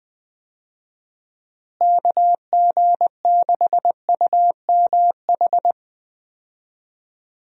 Morse code, heard as a keyed tone, about 20 wpm, 700 Hz. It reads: KG6UMH
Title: USA Call Signs - 6 Characters - Repeated 2x - 20wpm